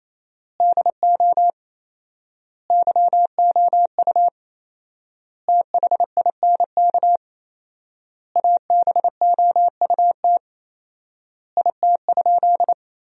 Morse code, heard as a keyed tone, about 28 wpm, 700 Hz. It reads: DO YOU THINK ABOUT IT?
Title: 4x Word-Spacing - Sentences from Top 200 Words - 28wpm